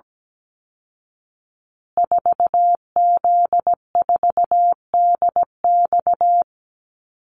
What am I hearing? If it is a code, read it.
Morse code, heard as a keyed tone, about 17 wpm, 700 Hz. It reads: 4Z4DX
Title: Call Signs Repeated 2x - 17wpm